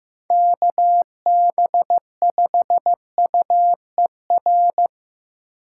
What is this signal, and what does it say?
Morse code, heard as a keyed tone, about 17 wpm, 700 Hz. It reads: KB5UER
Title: Speed Racing - USA 2x3 Call Signs 15wpm